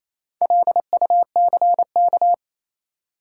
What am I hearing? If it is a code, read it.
Morse code, heard as a keyed tone, about 28 wpm, 700 Hz. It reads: LUCK
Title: Sentences from Top 1500 Words - New Word Drill - 28wpm